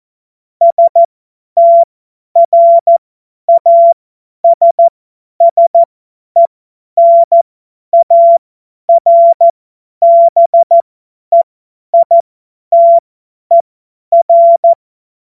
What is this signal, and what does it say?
Morse code, heard as a keyed tone, about 14 wpm, spaced as for 10 wpm, 680 Hz. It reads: STRASSENARBEITER
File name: Morsekurs / Witze / Witz 30. Dezember 2025 50 F70